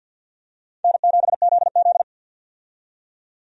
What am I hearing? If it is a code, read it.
Morse code, heard as a keyed tone, about 46 wpm, 700 Hz. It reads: N7ZZ
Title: USA Call Signs - 4 Characters - 50wpm